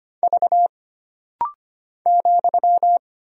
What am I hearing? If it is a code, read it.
Morse code, heard as a keyed tone, about 25 wpm, 700 Hz. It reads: V ,
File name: Single Character (Letter:Number:Punctuation:Prosign) - Rapid-Fire - 25wpm